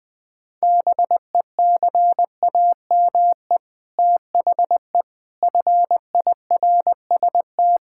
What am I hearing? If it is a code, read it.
Morse code, heard as a keyed tone, about 21 wpm, 700 Hz. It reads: BECAME THE FIRST
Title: Sets of 8 Words - 20wpm